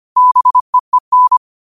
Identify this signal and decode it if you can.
Morse code, heard as a keyed tone, about 25 wpm, 1000 Hz. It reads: BEEN
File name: qotdfiles / 25wpm